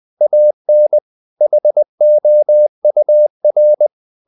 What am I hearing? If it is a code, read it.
Morse code, heard as a keyed tone, about 20 wpm, 600 Hz. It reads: AN HOUR
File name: the old man and the sea part 2 20wpm